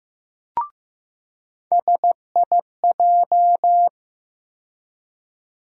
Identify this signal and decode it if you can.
Morse code, heard as a keyed tone, about 15 wpm, 700 Hz. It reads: SIJ
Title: USA Call Sign Postfixes - 3 Letters 15wpm